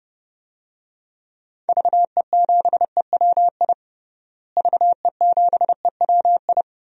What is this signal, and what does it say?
Morse code, heard as a keyed tone, about 30 wpm, 700 Hz. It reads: VE7EWI VE7EWI
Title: DXCC-100 Call Signs - 6 Characters - Repeated 2x - 30wpm